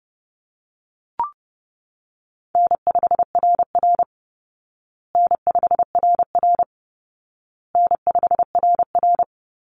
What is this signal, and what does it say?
Morse code, heard as a keyed tone, about 30 wpm, 700 Hz. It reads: N5RR N5RR N5RR
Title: Easy Call Signs Repeated 3x - 30wpm